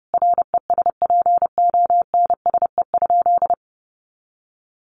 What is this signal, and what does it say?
Morse code, heard as a keyed tone, about 30 wpm, 700 Hz. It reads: RESPONSE?
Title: Sentences from Top 1400 Words 30wpm